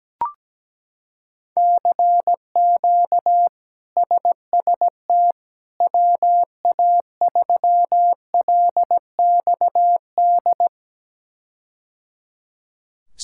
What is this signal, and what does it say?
Morse code, heard as a keyed tone, about 17 wpm, 700 Hz. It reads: CQ SST WA3LXD
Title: K1USN SST QSO Practice - 17wpm